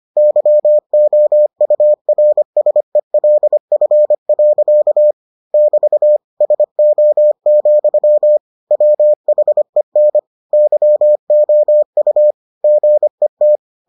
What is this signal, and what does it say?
Morse code, heard as a keyed tone, about 25 wpm, 600 Hz. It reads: YOURSELF. <BT> SO, WHEN YOU GET